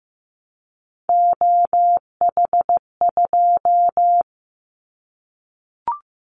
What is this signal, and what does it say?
Morse code, heard as a keyed tone, about 15 wpm, 700 Hz. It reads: OH2
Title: Callsign Prefixes for Top 25 DXCC - Repeated 3x - 15wpm